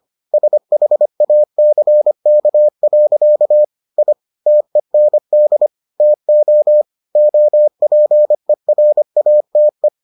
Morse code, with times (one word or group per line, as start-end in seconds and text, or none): 0.3-3.6 SHACK.
4.0-4.1 I
4.5-5.7 TEND
6.0-6.8 TO
7.2-9.9 OPERATE